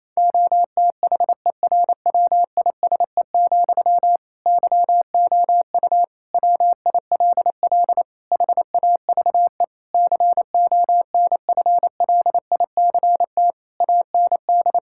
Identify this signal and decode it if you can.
Morse code, heard as a keyed tone, about 28 wpm, 700 Hz. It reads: OTHERWISE, YOU WILL HAVE CONFLICT AND
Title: Sentences from Top 1600 Words 28wpm